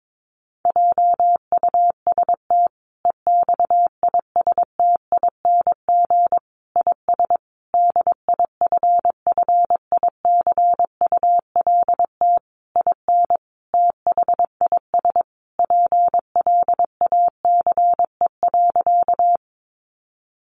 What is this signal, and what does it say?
Morse code, heard as a keyed tone, about 22 wpm, 700 Hz. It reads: JUST EXISTING IS DIFFICULT IN THIS PLACE.
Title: Sentences from Top 1500 Words 22wpm